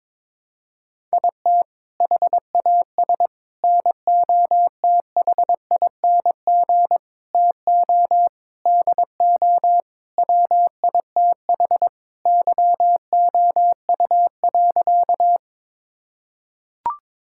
Morse code, heard as a keyed tone, about 22 wpm, 700 Hz. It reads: IT HAS NOTHING TO DO WITH YOU.
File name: practice / 22wpm / Sentences with 4-Word Phrases - 22wpm